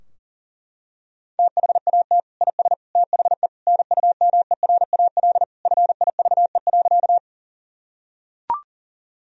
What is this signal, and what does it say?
Morse code, heard as a keyed tone, about 40 wpm, 700 Hz. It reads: THAT IS THE NUMERAL FIVE.
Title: Sentences from Top 1000 Words - Review Part 2 - 40wpm